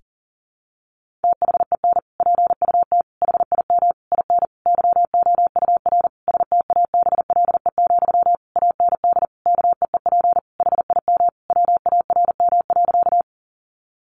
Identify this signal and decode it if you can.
Morse code, heard as a keyed tone, about 40 wpm, 700 Hz. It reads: THEN PUT HIM IN YOUR STABLE, AND KEEP HIM WARM.
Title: Sentences from Top 1500 Words 40wpm